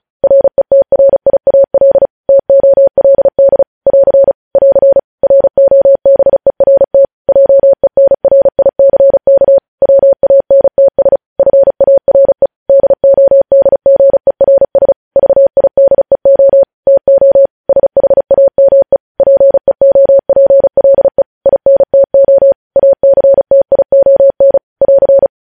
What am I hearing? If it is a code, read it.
Morse code, heard as a keyed tone, about 36 wpm, 550 Hz. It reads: RETRIAL TOLD <AR> <AR> ROBERT JENRICK WANTS FARE DODGERS VIDEO TO SHAME PEOPLE INTO ACTION <AR>
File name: podcast / News headlines-1x-35x35